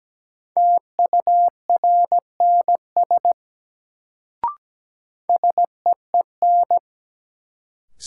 Morse code, heard as a keyed tone, about 17 wpm, 700 Hz. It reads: TURNS SEEN
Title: Sets of 5 Words Encore - New Word Drill - 17wpm